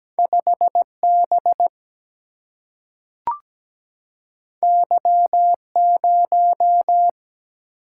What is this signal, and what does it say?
Morse code, heard as a keyed tone, about 17 wpm, 700 Hz. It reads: 5B Y0
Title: Single-Letter Single-Number 17wpm